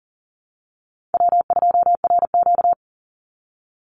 Morse code, heard as a keyed tone, about 41 wpm, 700 Hz. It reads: W2RQ
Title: Call Signs - 40wpm